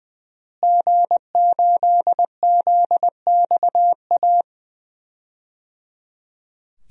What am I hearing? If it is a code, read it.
Morse code, heard as a keyed tone, about 21 wpm, 700 Hz. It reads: G8ZXA
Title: DXCC-100 Call Signs - 5 Characters - Repeated 3x - 20wpm